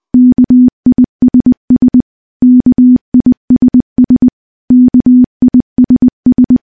Morse code, heard as a keyed tone, about 20 wpm, 250 Hz. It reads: KISS KISS KISS
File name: morse-training-20WPM